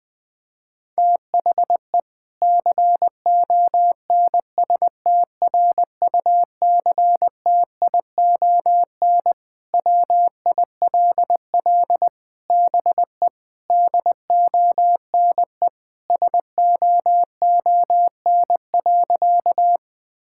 Morse code, heard as a keyed tone, about 20 wpm, 700 Hz. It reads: THE CONSTRUCTION WILL BE DONE SOON.
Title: Sentences from Top 1600 Words 20wpm